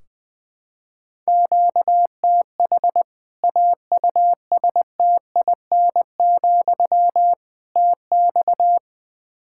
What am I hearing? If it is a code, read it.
Morse code, heard as a keyed tone, about 20 wpm, 700 Hz. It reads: QTH AUSTIN, TX
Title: Practice QSO - Exchange 1 - Beginning - 20wpm